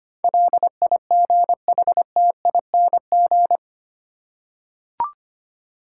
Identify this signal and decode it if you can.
Morse code, heard as a keyed tone, about 25 wpm, 700 Hz. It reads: LIGHTING
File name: Eight Letter Words - 25wpm